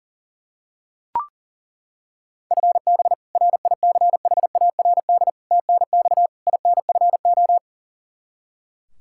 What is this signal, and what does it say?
Morse code, heard as a keyed tone, about 40 wpm, 700 Hz. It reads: FB RICHARD TNX INFO
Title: Practice QSO - Exchange 3 - More Info - 40wpm